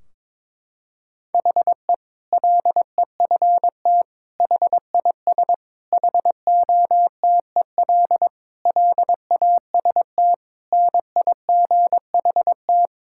Morse code, heard as a keyed tone, about 22 wpm, 700 Hz. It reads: HE LEFT HIS HOTEL LAST NIGHT
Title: Sentences from Top 1600 Words 22wpm